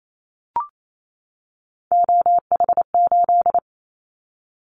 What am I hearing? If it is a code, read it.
Morse code, heard as a keyed tone, about 28 wpm, 700 Hz. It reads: OH8
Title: Callsign Prefixes for Top 25 DXCC - 28wpm